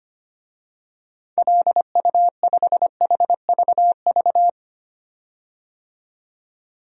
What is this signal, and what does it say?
Morse code, heard as a keyed tone, about 25 wpm, 700 Hz. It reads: LU5HVV
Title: DXCC-100 Call Signs - 6 Characters - 25wpm